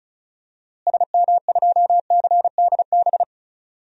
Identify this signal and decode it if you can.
Morse code, heard as a keyed tone, about 34 wpm, 700 Hz. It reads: SM2CDB
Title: DXCC-100 Call Signs - 6 Characters - Repeated 2x - 35wpm